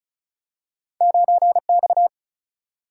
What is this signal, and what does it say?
Morse code, heard as a keyed tone, about 36 wpm, 700 Hz. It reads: 9 X
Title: Single-Letter Single-Number 35wpm